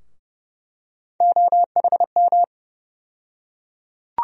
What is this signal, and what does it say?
Morse code, heard as a keyed tone, about 30 wpm, 700 Hz. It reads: OHM
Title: USA Call Sign Postfixes - 3 Letters 30wpm